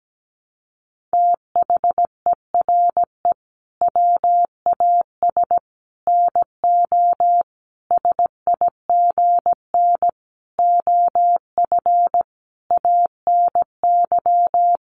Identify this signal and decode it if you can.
Morse code, heard as a keyed tone, about 17 wpm, 700 Hz. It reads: THERE WAS NO SIGN OF ANY
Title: Sentences from Top 1000 Words 17wpm